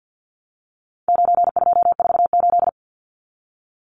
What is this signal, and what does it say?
Morse code, heard as a keyed tone, about 50 wpm, 700 Hz. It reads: 9248
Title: Numbers - 4-Digit 50wpm